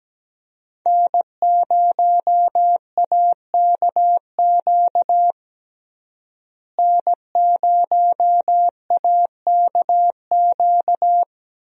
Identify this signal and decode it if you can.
Morse code, heard as a keyed tone, about 17 wpm, 700 Hz. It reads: N0AKQ N0AKQ
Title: USA Call Signs - 5 Characters - Repeated 3x - 17wpm